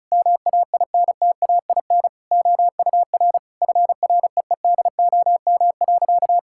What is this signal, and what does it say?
Morse code, heard as a keyed tone, about 34 wpm, 700 Hz. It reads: MAINTAIN OUR FREEDOM.
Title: Sentences from Top 1000 Words - Review Part 5 - 35wpm